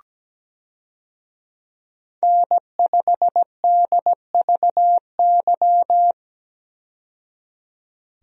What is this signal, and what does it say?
Morse code, heard as a keyed tone, about 17 wpm, 700 Hz. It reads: N5DVY
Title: USA Call Signs - 5 Characters - 17wpm